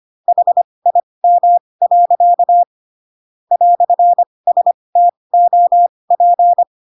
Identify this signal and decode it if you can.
Morse code, heard as a keyed tone, about 25 wpm, spaced as for 20 wpm, 700 Hz. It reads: HIM. "STOP